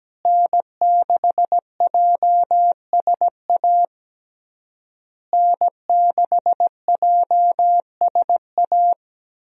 Morse code, heard as a keyed tone, about 17 wpm, 700 Hz. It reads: N6JSA N6JSA
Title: USA Call Signs - 5 Characters - Repeated 3x - 17wpm